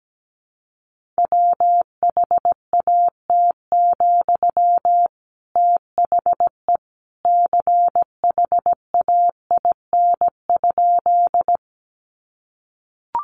Morse code, heard as a keyed tone, about 17 wpm, 700 Hz. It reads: WHAT, THE CHAIN?
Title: Sentences from Top 1500 Words 17wpm